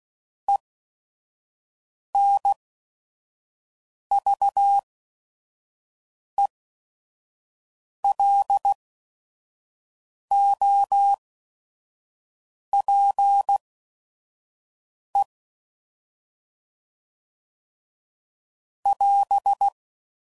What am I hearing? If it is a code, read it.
Morse code, heard as a keyed tone, about 16 wpm, spaced as for 5 wpm, 800 Hz. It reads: ENVELOPE <AS>